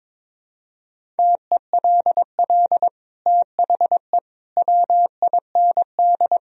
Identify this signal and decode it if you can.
Morse code, heard as a keyed tone, about 21 wpm, 700 Hz. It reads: TELL THE WIND
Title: Sentences from Top 1000 Words - Review Part 2 - 22wpm